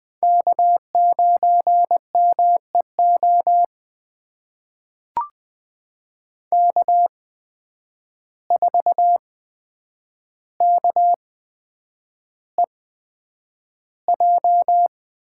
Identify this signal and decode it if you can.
Morse code, heard as a keyed tone, about 20 wpm, spaced as for 9 wpm, 700 Hz. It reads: K9MEO K4KEJ
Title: USA Call Signs - 5 Characters - 8x Character Spacing - 20wpm